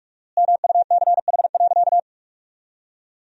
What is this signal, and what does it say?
Morse code, heard as a keyed tone, about 42 wpm, 700 Hz. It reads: MUCH.
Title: 4x Word-Spacing - Sentences from Top 500 Words - 45wpm